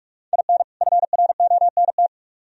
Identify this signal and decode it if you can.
Morse code, heard as a keyed tone, about 42 wpm, 700 Hz. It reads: IN FRONT